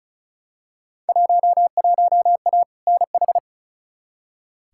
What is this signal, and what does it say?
Morse code, heard as a keyed tone, about 34 wpm, 700 Hz. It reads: 11A NH
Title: Field Day Exchanges Encore - 35wpm